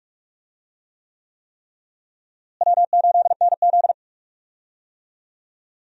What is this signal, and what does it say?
Morse code, heard as a keyed tone, about 43 wpm, 700 Hz. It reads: W8NZ
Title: USA Call Signs - 4 Characters - Repeated 3x - 45wpm